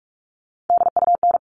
Morse code, heard as a keyed tone, about 44 wpm, 700 Hz. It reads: DUN
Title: All Cut Numbers - 3-Digit 45wpm